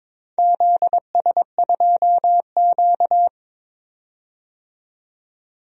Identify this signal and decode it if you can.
Morse code, heard as a keyed tone, about 22 wpm, 690 Hz. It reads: ZS2Q
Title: DXCC-100 Call Signs - 4 Characters - 22wpm